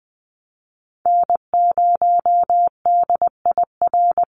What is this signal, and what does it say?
Morse code, heard as a keyed tone, about 20 wpm, 700 Hz. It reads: N0DIR
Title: Call Signs Repeated 2x - 20wpm